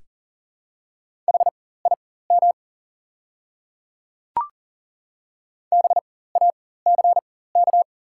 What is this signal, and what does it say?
Morse code, heard as a keyed tone, about 40 wpm, spaced as for 19 wpm, 700 Hz. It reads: HIM BACK
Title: 4x Character Spacing - Top 300 Words - 40wpm